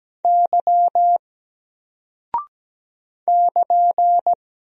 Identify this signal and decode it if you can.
Morse code, heard as a keyed tone, about 17 wpm, 700 Hz. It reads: Y <KN>
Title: Single Character (Letter:Number:Punctuation:Prosign) 17wpm